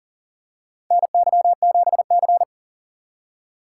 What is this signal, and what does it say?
Morse code, heard as a keyed tone, about 39 wpm, 700 Hz. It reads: NY7C
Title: USA Call Signs - 4 Characters - 4x Character Spacing - 40wpm